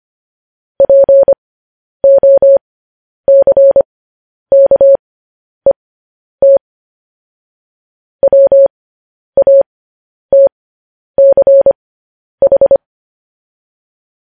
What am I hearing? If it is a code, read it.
Morse code, heard as a keyed tone, about 25 wpm, spaced as for 10 wpm, 550 Hz. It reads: POCKET WATCH
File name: podcast / News headlines-1x-25x10